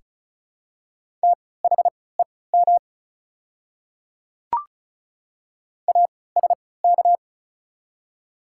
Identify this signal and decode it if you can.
Morse code, heard as a keyed tone, about 36 wpm, spaced as for 20 wpm, 700 Hz. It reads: THEM ASK
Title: 3x Character Spacing - Top 200 Words - 35wpm